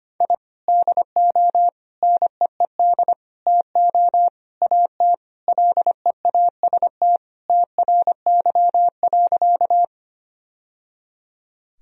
Morse code, heard as a keyed tone, about 25 wpm, 700 Hz. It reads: I DO NEED TO AT LEAST TRY.
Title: Sentences from Top 1000 Words - Review Part 1 - 25wpm